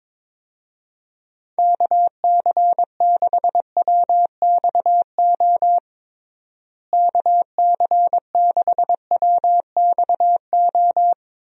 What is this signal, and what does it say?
Morse code, heard as a keyed tone, about 23 wpm, 700 Hz. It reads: KC6WXO KC6WXO
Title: USA Call Signs - 6 Characters - Repeated 2x - 22wpm